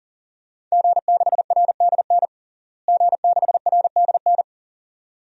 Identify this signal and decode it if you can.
Morse code, heard as a keyed tone, about 40 wpm, 700 Hz. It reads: G6RDN G6RDN
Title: DXCC-100 Call Signs - 5 Characters - Repeated 2x - 40wpm